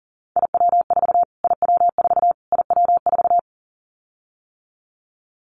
Morse code, heard as a keyed tone, about 40 wpm, 700 Hz. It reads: IW4 IW4 IW4
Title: Callsign Prefixes for Top 25 DXCC - Repeated 3x - 40wpm